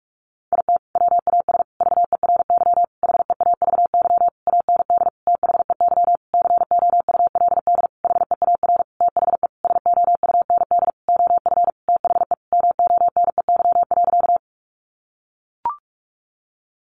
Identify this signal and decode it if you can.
Morse code, heard as a keyed tone, about 44 wpm, 700 Hz. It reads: IT WAS VERY HEAVY AND THEY COULD HEAR THE SOUND OF THE MONEY.
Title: Sentences from Top 1100 Words 45wpm